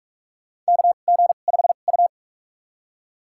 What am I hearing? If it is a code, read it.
Morse code, heard as a keyed tone, about 46 wpm, spaced as for 32 wpm, 700 Hz. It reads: KG5U
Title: USA Call Signs - 4 Characters - 2x Character Spacing - 45wpm